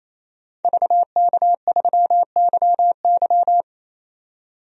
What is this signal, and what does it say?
Morse code, heard as a keyed tone, about 28 wpm, 700 Hz. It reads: VK3YY
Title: DXCC-100 Call Signs - 5 Characters - 28wpm